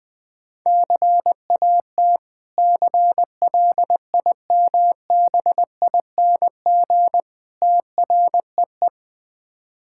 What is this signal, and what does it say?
Morse code, heard as a keyed tone, about 20 wpm, 690 Hz. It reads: CAT CLIMBING TREE
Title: Difficult Words with Hints - 20wpm